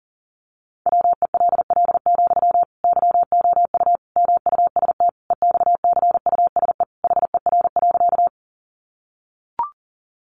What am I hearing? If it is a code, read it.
Morse code, heard as a keyed tone, about 40 wpm, 700 Hz. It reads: WELL, YOU MUST EXCUSE HER.
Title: Sentences from Top 700 Words 40wpm